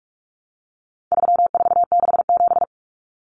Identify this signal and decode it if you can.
Morse code, heard as a keyed tone, about 44 wpm, 700 Hz. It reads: 3467
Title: Numbers - 4-Digit 45wpm